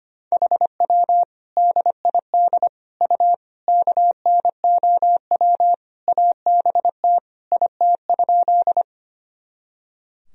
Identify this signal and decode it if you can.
Morse code, heard as a keyed tone, about 25 wpm, 700 Hz. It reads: HW DID U KNOW ABT IT?
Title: Sentences from Top 100 Words and Abbreviations 25wpm